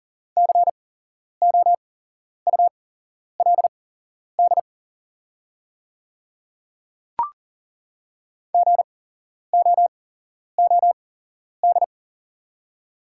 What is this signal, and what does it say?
Morse code, heard as a keyed tone, about 40 wpm, spaced as for 11 wpm, 690 Hz. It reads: COULD GOOD
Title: 8x Character Spacing - Top 300 Words - 40wpm